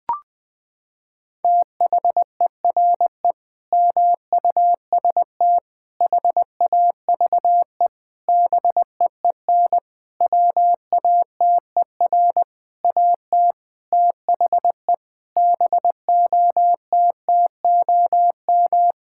Morse code, heard as a keyed tone, about 21 wpm, 700 Hz. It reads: THERE MUST HAVE BEEN WATER AT THE BOTTOM
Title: Sentences from Top 1000 Words - Review Part 6 - 20wpm